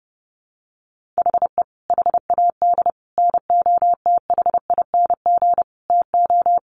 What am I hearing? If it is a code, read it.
Morse code, heard as a keyed tone, about 30 wpm, 700 Hz. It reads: HE HAD NOTHING TO
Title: Sentences from Top 1100 Words 30wpm